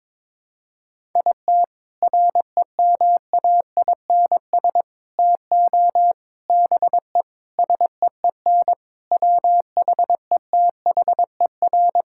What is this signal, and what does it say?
Morse code, heard as a keyed tone, about 22 wpm, 700 Hz. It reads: IT REMAINS TO BE SEEN WHETHER
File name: practice / 22wpm / Sets of 8 Words - 22wpm